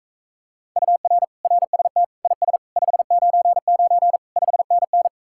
Morse code, heard as a keyed tone, about 45 wpm, 700 Hz. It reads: UR RST IS 599 5NN